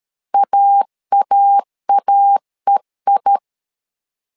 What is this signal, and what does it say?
Morse code, heard as a keyed tone, about 13 wpm, 800 Hz. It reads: AAAEI
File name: LEZIONI / 2 UHDN / 2 UH 12wpm 1439